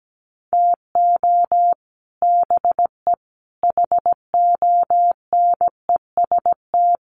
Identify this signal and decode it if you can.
Morse code, heard as a keyed tone, about 17 wpm, 700 Hz. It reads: TO BE HONEST
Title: Sets of 5 Words Encore 17wpm